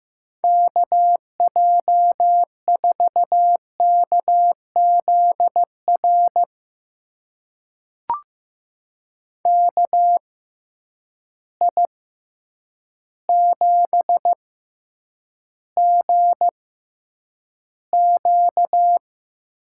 Morse code, heard as a keyed tone, about 15 wpm, spaced as for 8 wpm, 700 Hz. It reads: KJ4KZR KI7GQ
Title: USA Call Signs - 6 Characters - 6x Character Spacing - 15wpm